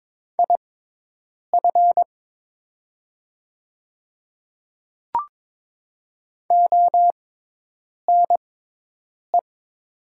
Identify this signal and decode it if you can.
Morse code, heard as a keyed tone, about 22 wpm, spaced as for 8 wpm, 700 Hz. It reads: IF ONE
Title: 6x Character Spacing - Top 500 Words - 22wpm